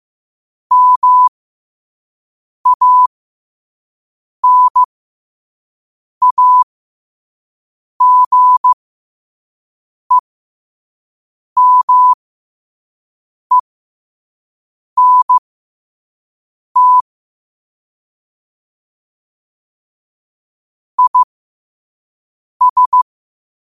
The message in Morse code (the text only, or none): MANAGEMENT IS